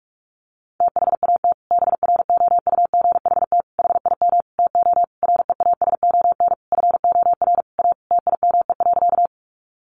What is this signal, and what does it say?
Morse code, heard as a keyed tone, about 44 wpm, 700 Hz. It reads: THAT BROUGHT HIM TO REASON FOR A TIME.
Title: Sentences from Top 700 Words 45wpm